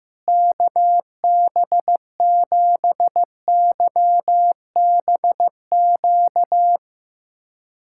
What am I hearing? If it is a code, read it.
Morse code, heard as a keyed tone, about 16 wpm, 700 Hz. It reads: KB7YBQ
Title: USA Callsign DE Callsign - 6 Characters - 15wpm